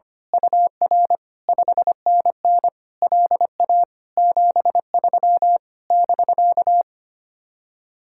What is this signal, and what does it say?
Morse code, heard as a keyed tone, about 24 wpm, 700 Hz. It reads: UR 5NN LA 73 <BK>
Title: Speed-Racing Common QSO Phrases - 25wpm